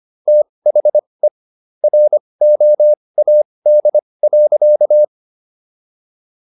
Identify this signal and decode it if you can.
Morse code, heard as a keyed tone, about 25 wpm, spaced as for 20 wpm, 600 Hz. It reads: THE ROAD.